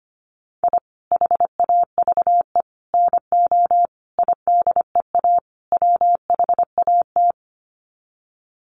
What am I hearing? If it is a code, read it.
Morse code, heard as a keyed tone, about 25 wpm, 700 Hz. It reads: I HAVE NO IDEA WHAT
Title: Sets of 5 Words Encore 25wpm